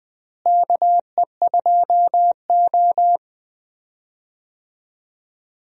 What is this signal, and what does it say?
Morse code, heard as a keyed tone, about 20 wpm, 700 Hz. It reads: KE2O
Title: USA Call Signs - 4 Characters - Repeated 2x - 20wpm